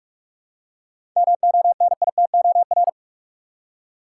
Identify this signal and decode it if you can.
Morse code, heard as a keyed tone, about 47 wpm, 700 Hz. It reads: MONITOR
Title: Seven Letter Words - 45wpm